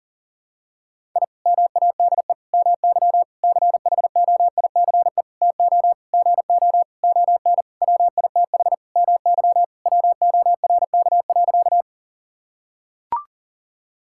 Morse code, heard as a keyed tone, about 40 wpm, 700 Hz. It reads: I MADE MY CHOICE TO GO ON WITH MY WORK.
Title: Sentences from Top 1000 Words - Review Part 6 - 40wpm